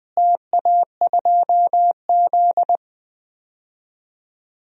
Morse code, heard as a keyed tone, about 20 wpm, 700 Hz. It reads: TA2Z
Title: DXCC-100 Call Signs - 4 Characters - Repeated 3x - 20wpm